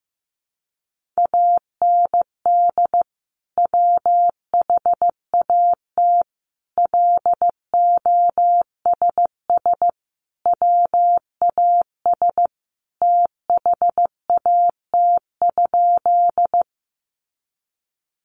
Morse code, heard as a keyed tone, about 15 wpm, 700 Hz. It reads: AND WHAT LOSS WAS THAT?
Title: Sentences from Top 1400 Words 15wpm